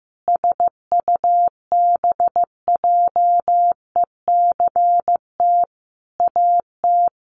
Morse code, heard as a keyed tone, about 14 wpm, 700 Hz. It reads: SUBJECT AT
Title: Sentences from Top 1000 Words 15wpm